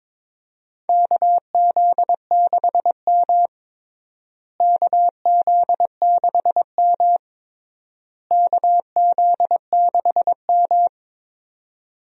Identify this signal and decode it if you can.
Morse code, heard as a keyed tone, about 22 wpm, 700 Hz. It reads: KZ6M KZ6M KZ6M
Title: USA Call Signs - 4 Characters - Repeated 3x - 22wpm